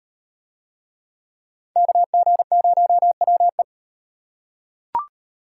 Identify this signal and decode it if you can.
Morse code, heard as a keyed tone, about 36 wpm, 700 Hz. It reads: KG0WE